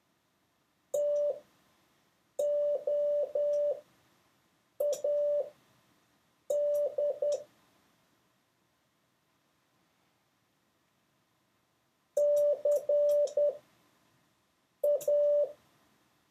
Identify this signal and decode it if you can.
Morse code, heard as a keyed tone, about 10 wpm, spaced as for 5 wpm, 580 Hz. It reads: TOAD CA
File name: 6of6-test